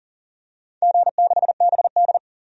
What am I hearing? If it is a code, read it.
Morse code, heard as a keyed tone, about 40 wpm, 700 Hz. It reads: G6BD
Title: DXCC-100 Call Signs - 4 Characters - 40wpm